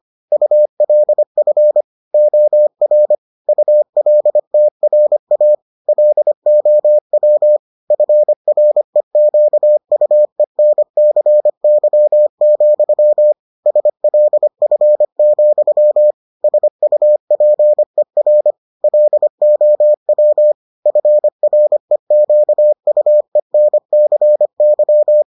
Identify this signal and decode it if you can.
Morse code, heard as a keyed tone, about 25 wpm, 600 Hz. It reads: ULF OR ULTRA LOW FREQUENCY, SLF, SUPER LOW FREQUENCY